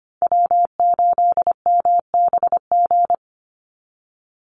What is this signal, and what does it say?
Morse code, heard as a keyed tone, about 25 wpm, 700 Hz. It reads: W8MBG
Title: USA Call Signs 25wpm